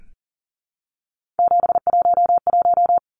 Morse code, heard as a keyed tone, about 40 wpm, 700 Hz. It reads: 711
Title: Numbers - 3-Digit 40wpm